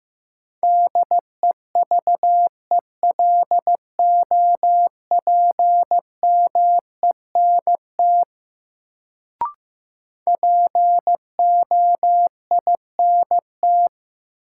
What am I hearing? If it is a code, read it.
Morse code, heard as a keyed tone, about 15 wpm, 700 Hz. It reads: DEVELOPMENT POINT
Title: Sets of 3 Words - New Word Drill - 15wpm